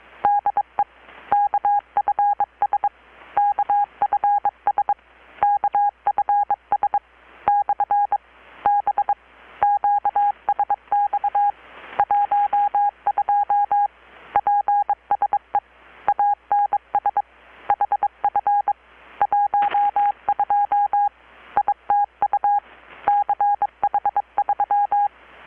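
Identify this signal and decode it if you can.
Morse code, heard as a keyed tone, about 22 wpm, 820 Hz. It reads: DE KFS KFS KFS/B QSX 12 PSE ANS HF 12 ITU CH3
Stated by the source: Morse from KPH's Night of Nights XX maritime radiotelegraphy event, received in Chiapas, Mexico about 3,800 km away on a Kenwood R-600 with a grounded 10 m long-wire antenna.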